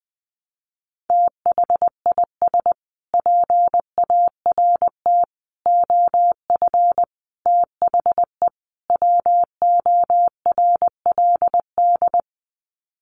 Morse code, heard as a keyed tone, about 20 wpm, 700 Hz. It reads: THIS PART OF THE WORLD
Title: Sets of 5 Words Encore2 20wpm